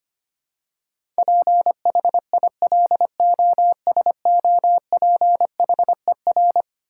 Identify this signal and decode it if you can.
Morse code, heard as a keyed tone, about 25 wpm, 700 Hz. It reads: PHILOSOPHER
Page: morse-code-ninja-practice-25wpm